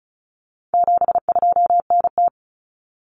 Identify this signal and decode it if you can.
Morse code, heard as a keyed tone, about 34 wpm, 700 Hz. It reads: 72NT
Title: Common Cut Numbers - 4-Digit 35wpm